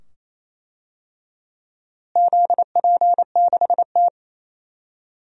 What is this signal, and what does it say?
Morse code, heard as a keyed tone, about 28 wpm, 700 Hz. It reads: ZP6T
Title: DXCC-100 Call Signs - 4 Characters - Repeated 2x - 28wpm